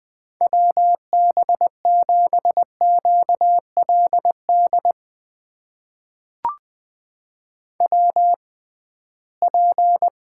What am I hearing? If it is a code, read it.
Morse code, heard as a keyed tone, about 20 wpm, spaced as for 15 wpm, 700 Hz. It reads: WB7QLD WP
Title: USA Call Signs - 6 Characters - 6x Character Spacing - 20wpm